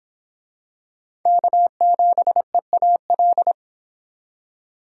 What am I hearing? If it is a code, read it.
Morse code, heard as a keyed tone, about 25 wpm, 700 Hz. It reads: K7EAL